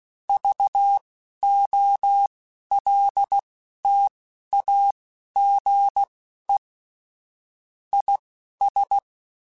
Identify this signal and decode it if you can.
Morse code, heard as a keyed tone, about 16 wpm, spaced as for 11 wpm, 800 Hz. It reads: VOLTAGE IS
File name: Technical Subjects / Technical Subjects-mp3s / XC @10